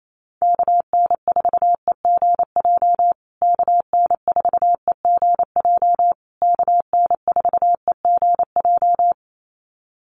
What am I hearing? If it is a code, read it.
Morse code, heard as a keyed tone, about 28 wpm, 700 Hz. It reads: KN4EGJ KN4EGJ KN4EGJ
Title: USA Call Signs Repeated 3x 28wpm